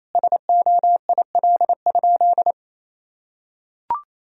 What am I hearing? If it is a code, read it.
Morse code, heard as a keyed tone, about 28 wpm, 700 Hz. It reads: SOIL?
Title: Sentences from Top 1000 Words - Review Part 4 - 28wpm